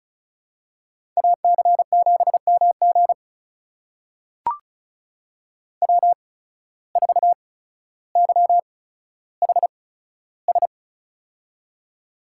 Morse code, heard as a keyed tone, about 36 wpm, spaced as for 15 wpm, 700 Hz. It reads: AC7MG W4YHS
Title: USA Call Signs - 8x Character Spacing - 35wpm